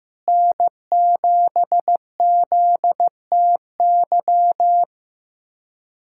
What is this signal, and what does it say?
Morse code, heard as a keyed tone, about 16 wpm, 700 Hz. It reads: N7ZTY
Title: USA Callsign DE Callsign - 5 Characters - 15wpm